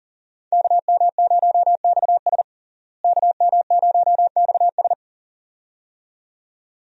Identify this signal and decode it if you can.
Morse code, heard as a keyed tone, about 40 wpm, 700 Hz. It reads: KM0XS KM0XS
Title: USA Call Signs - 5 Characters - Repeated 2x - 40wpm